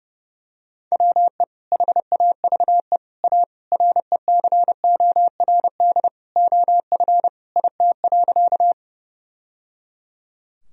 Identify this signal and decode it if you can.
Morse code, heard as a keyed tone, about 30 wpm, 700 Hz. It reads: WE HAVE A RECORD OF IT.
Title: Sentences from Top 1000 Words - Review Part 3 - 30wpm